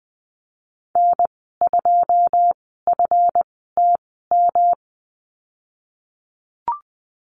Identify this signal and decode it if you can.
Morse code, heard as a keyed tone, about 19 wpm, spaced as for 14 wpm, 700 Hz. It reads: N2FTM
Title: 173 Lesson 28g - Number 9 - Call Signs 20wpm